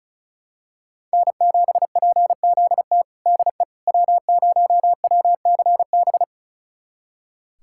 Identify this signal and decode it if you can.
Morse code, heard as a keyed tone, about 36 wpm, 700 Hz. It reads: N7PZT DE W0WCB
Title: USA Callsign DE Callsign 35wpm